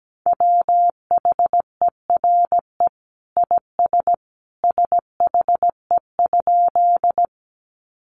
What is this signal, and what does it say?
Morse code, heard as a keyed tone, about 17 wpm, 700 Hz. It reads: WHERE IS SHE?
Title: Sentences from Top 200 Words 17wpm